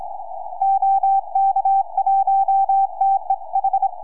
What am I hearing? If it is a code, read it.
Morse code, heard as a keyed tone, about 26 wpm, 780 Hz. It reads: OK1TEH
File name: es3rf 4m